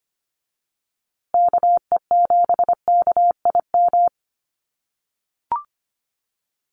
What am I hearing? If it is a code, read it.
Morse code, heard as a keyed tone, about 25 wpm, 700 Hz. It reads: KE7KIM
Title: USA Call Signs Repeated 3x 25wpm